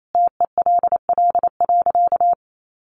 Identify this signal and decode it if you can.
Morse code, heard as a keyed tone, about 28 wpm, 700 Hz. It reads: TELL.
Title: A Sound Of Thunder 28wpm